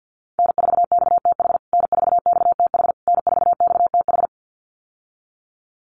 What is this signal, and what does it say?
Morse code, heard as a keyed tone, about 50 wpm, 700 Hz. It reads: N4XTH N4XTH N4XTH
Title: USA Call Signs Repeated 3x 50wpm